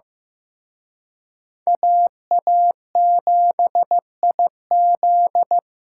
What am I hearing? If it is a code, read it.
Morse code, heard as a keyed tone, about 18 wpm, 700 Hz. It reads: AA7IZ
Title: Speed Racing - USA 2x2 Call Signs 15wpm